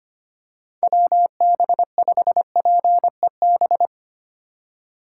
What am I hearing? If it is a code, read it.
Morse code, heard as a keyed tone, about 25 wpm, 700 Hz. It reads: WB5PEB
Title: USA Call Signs - 6 Characters - 3x Character Spacing - 25wpm